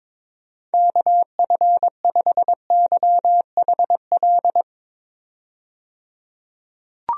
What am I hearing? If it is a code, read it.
Morse code, heard as a keyed tone, about 22 wpm, 700 Hz. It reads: KF5YHL
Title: USA Call Signs - 6 Characters - 22wpm